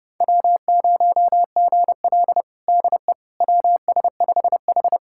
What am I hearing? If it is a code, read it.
Morse code, heard as a keyed tone, about 30 wpm, 700 Hz. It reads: W0GL DE WS5H
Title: USA Callsign DE Callsign - 4 Characters - 30wpm